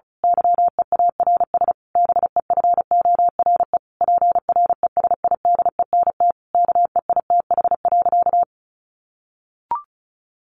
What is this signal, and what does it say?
Morse code, heard as a keyed tone, about 34 wpm, 700 Hz. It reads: YEARS BEFORE PRESIDENT KEITH.
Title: A Sound Of Thunder 35wpm